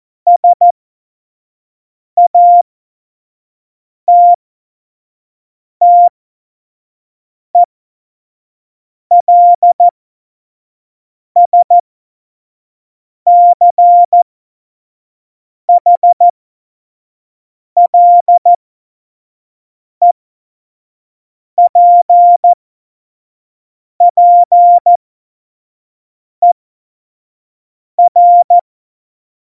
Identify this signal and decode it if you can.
Morse code, heard as a keyed tone, about 14 wpm, spaced as for 5 wpm, 700 Hz. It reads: SATTELSCHLEPPER
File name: Morsekurs / Witze / Witz 09. April 2019, 25 F70